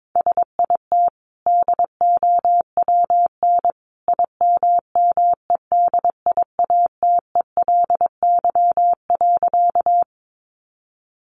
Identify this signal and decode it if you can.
Morse code, heard as a keyed tone, about 22 wpm, 700 Hz. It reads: SIT DOWN IMMEDIATELY.
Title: Sentences from Top 1100 Words 22wpm